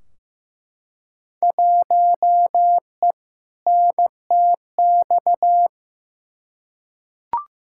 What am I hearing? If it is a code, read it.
Morse code, heard as a keyed tone, about 15 wpm, 700 Hz. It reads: 1E NTX
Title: Field Day Exchanges - 15wpm